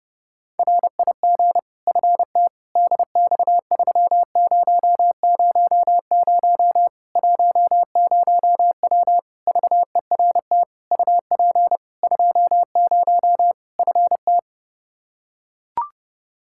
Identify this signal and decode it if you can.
Morse code, heard as a keyed tone, about 32 wpm, 700 Hz. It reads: RIG FT DX3000 10W VERT UP 20 FT
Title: Practice QSO - Exchange 2 - Info - 30wpm